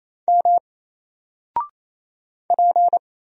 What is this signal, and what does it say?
Morse code, heard as a keyed tone, about 27 wpm, 700 Hz. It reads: M P
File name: Single-Letter-Number-28wpm